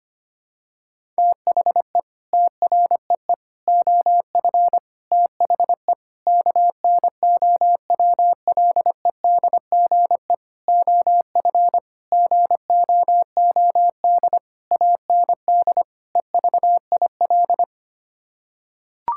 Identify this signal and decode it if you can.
Morse code, heard as a keyed tone, about 25 wpm, 700 Hz. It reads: THE TREE OF THE KNOWLEDGE OF GOOD AND EVIL
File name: Sets of 9 Words - 25wpm